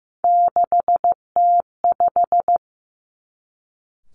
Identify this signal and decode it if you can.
Morse code, heard as a keyed tone, about 15 wpm, 700 Hz. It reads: 6T5
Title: Three Challenging Letter-Number Combinations 15wpm